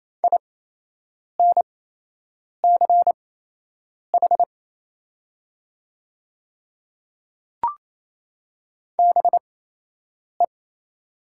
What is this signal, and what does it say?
Morse code, heard as a keyed tone, about 28 wpm, spaced as for 8 wpm, 700 Hz. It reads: INCH BE
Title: 8x Character Spacing - Top 500 Words - 28wpm